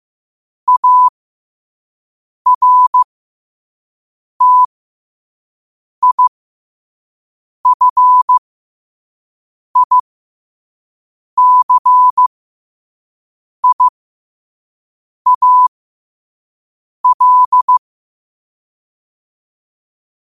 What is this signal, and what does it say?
Morse code, heard as a keyed tone, about 15 wpm, spaced as for 5 wpm, 1000 Hz. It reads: ARTIFICIAL
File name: qotdfiles / 5wpm